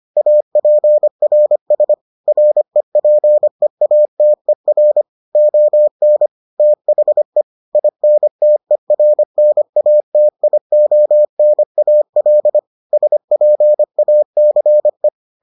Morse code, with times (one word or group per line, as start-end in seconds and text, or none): 0.2-1.9 APRS
2.3-5.0 REPEATER
5.4-6.3 ON
6.6-7.4 THE
7.8-12.6 INTERNATIONAL
12.9-15.1 SPACE